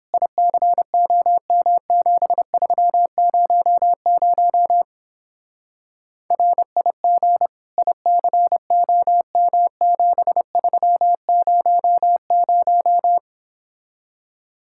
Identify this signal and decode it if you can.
Morse code, heard as a keyed tone, about 29 wpm, 700 Hz. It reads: ICOM7300 RIG ICOM7300